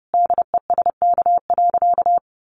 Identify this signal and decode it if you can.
Morse code, heard as a keyed tone, about 28 wpm, 700 Hz. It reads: DESK.
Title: Easy Phrases 30wpm